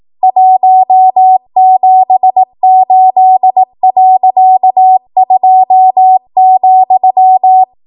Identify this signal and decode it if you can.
Morse code, heard as a keyed tone, about 18 wpm, 750 Hz. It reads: 178.2,
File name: Archive / 18 WPM / 18WPM